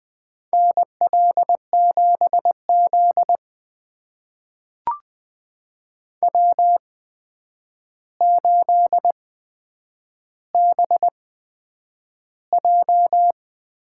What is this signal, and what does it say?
Morse code, heard as a keyed tone, about 20 wpm, spaced as for 9 wpm, 700 Hz. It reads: NL7Z W8BJ
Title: USA Call Signs - 4 Characters - 8x Character Spacing - 20wpm